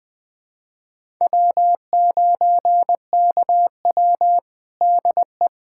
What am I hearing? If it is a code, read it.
Morse code, heard as a keyed tone, about 21 wpm, 700 Hz. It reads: W9KW DE
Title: USA Callsign DE Callsign 20wpm